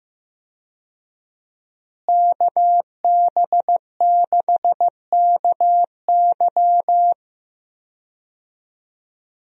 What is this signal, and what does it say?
Morse code, heard as a keyed tone, about 15 wpm, 700 Hz. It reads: KB6KY
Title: USA Call Signs - 5 Characters - Repeated 3x - 15wpm